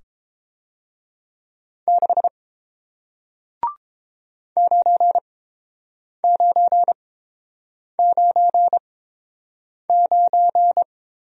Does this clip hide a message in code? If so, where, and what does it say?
Yes: Morse code from 1.9 to 10.8 s, heard as a keyed tone, about 28 wpm, spaced as for 7 wpm, 700 Hz. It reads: 6 9999